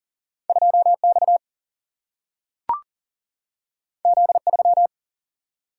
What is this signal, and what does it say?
Morse code, heard as a keyed tone, about 40 wpm, 700 Hz. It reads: 2X Z3
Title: Single-Letter Single-Number 40wpm